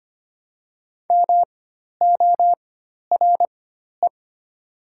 Morse code, heard as a keyed tone, about 25 wpm, spaced as for 12 wpm, 700 Hz. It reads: MORE
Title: 4x Character Spacing - Top 100 Words - 25wpm